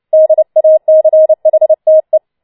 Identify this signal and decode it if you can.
Morse code, heard as a keyed tone, about 29 wpm, 630 Hz. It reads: DACHTE
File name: Schneewittchen Teil3 30wpm